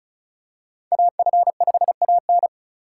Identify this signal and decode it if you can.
Morse code, heard as a keyed tone, about 36 wpm, 700 Hz. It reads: AF5AN
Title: USA Call Signs - 5 Characters - 3x Character Spacing - 35wpm